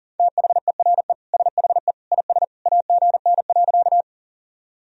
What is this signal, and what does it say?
Morse code, heard as a keyed tone, about 40 wpm, 700 Hz. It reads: THERE SHE IS AGN.
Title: Sentences from Top 200 Words and Abbreviations 40wpm